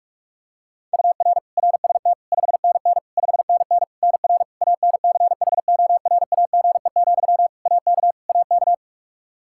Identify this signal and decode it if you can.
Morse code, heard as a keyed tone, about 46 wpm, 700 Hz. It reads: UR RST 5NN 5NN NR ANCHORAGE, AK AK
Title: Practice QSO - Exchange 1 - Beginning - 45wpm